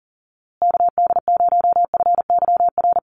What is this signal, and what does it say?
Morse code, heard as a keyed tone, about 39 wpm, 700 Hz. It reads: KD0FYR
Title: USA Call Signs Repeated 2x 40wpm